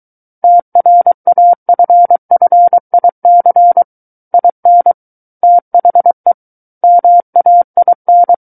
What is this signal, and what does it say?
Morse code, heard as a keyed tone, about 23 wpm, 700 Hz. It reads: TRAFFIC IN THE MAIN
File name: Book One / 14-Chapter 14 - 23 WPM